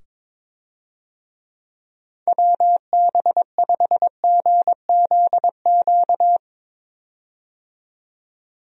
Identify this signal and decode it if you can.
Morse code, heard as a keyed tone, about 21 wpm, 700 Hz. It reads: WB5GZQ
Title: USA Call Signs - 6 Characters - 22wpm